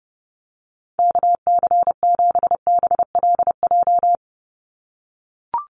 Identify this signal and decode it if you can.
Morse code, heard as a keyed tone, about 30 wpm, 700 Hz. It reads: KC7BLJ
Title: USA Call Signs 30wpm